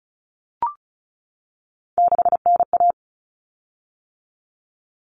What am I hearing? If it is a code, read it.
Morse code, heard as a keyed tone, about 34 wpm, 700 Hz. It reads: 6NA
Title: Common Cut Numbers - 3-Digit 35wpm